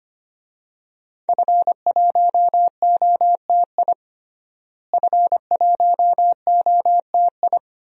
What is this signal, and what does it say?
Morse code, heard as a keyed tone, about 25 wpm, 700 Hz. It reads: F1OTI F1OTI
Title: DXCC-100 Call Signs - 5 Characters - Repeated 2x - 25wpm